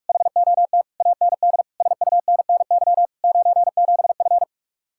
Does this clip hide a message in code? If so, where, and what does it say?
Yes: Morse code from 0.1 to 4.4 s, heard as a keyed tone, about 49 wpm, 700 Hz. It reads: HOT AND SUNNY 97F